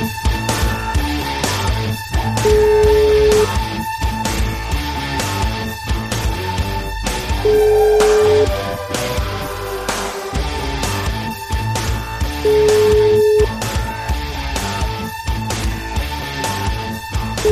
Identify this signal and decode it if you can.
Ringback tone (Europe, 425 Hz), with noise.